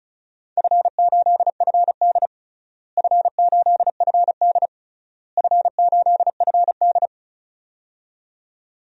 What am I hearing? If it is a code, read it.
Morse code, heard as a keyed tone, about 36 wpm, 700 Hz. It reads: F8FD F8FD F8FD
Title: DXCC-100 Call Signs - 4 Characters - Repeated 3x - 35wpm